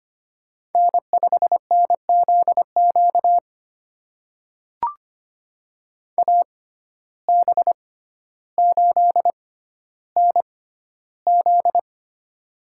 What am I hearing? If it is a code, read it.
Morse code, heard as a keyed tone, about 25 wpm, spaced as for 13 wpm, 700 Hz. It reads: N5NZQ AB8NZ
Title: USA Call Signs - 5 Characters - 6x Character Spacing - 25wpm